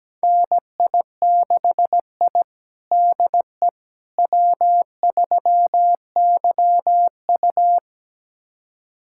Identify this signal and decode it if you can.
Morse code, heard as a keyed tone, about 17 wpm, 700 Hz. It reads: NI6I DE W3YU
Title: USA Callsign DE Callsign - 4 Characters - 17wpm